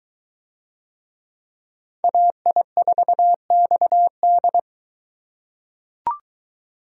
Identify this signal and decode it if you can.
Morse code, heard as a keyed tone, about 21 wpm, 700 Hz. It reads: AI4XD